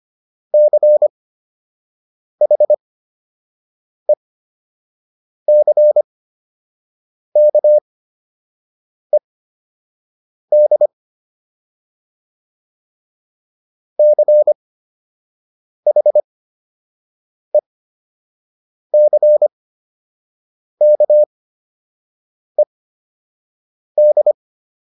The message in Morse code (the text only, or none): CHECKED CHECKED